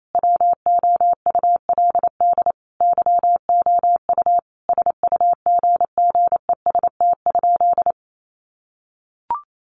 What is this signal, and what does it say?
Morse code, heard as a keyed tone, about 28 wpm, 700 Hz. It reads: WOULD YOU SUGGEST?
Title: Sentences from Top 1300 Words 28wpm